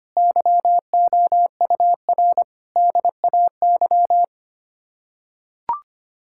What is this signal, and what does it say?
Morse code, heard as a keyed tone, about 25 wpm, 700 Hz. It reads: YOUR DAY
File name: Sets of 9 Words - 25wpm